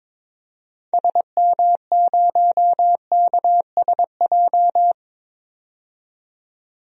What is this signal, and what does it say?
Morse code, heard as a keyed tone, about 22 wpm, 700 Hz. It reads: SM0KSJ
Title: DXCC-100 Call Signs - 6 Characters - 22wpm